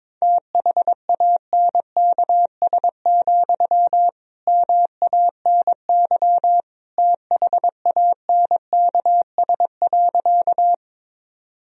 Morse code, heard as a keyed tone, about 22 wpm, 700 Hz. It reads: THANKS, MANY THANKS.
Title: Sentences from Top 1000 Words - Review Part 3 - 22wpm